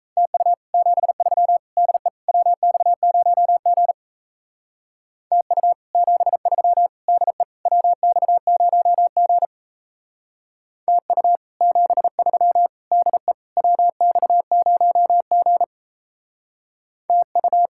Morse code, heard as a keyed tone, about 32 wpm, 700 Hz. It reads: TU 73 DE WX0G TU 73 DE WX0G TU 73 DE WX0G TU
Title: Speed-Racing Common QSO Phrases - 28wpm